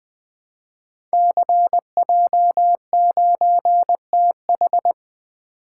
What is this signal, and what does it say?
Morse code, heard as a keyed tone, about 20 wpm, 700 Hz. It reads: CJ9TH
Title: Code Groups - 5 Letter:Numbers - 20wpm